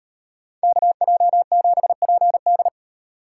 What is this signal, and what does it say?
Morse code, heard as a keyed tone, about 38 wpm, 700 Hz. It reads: KJ7PD